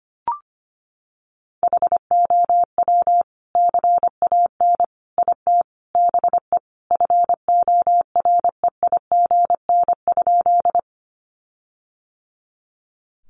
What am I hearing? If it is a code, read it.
Morse code, heard as a keyed tone, about 25 wpm, 700 Hz. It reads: HOW CAN IT BE FOREIGN?
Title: Sentences from Top 1400 Words 25wpm